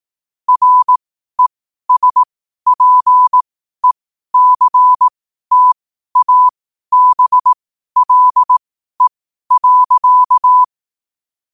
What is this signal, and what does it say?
Morse code, heard as a keyed tone, about 18 wpm, spaced as for 12 wpm, 1000 Hz. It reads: RESPECTABLE.
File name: qotdfiles / 12wpm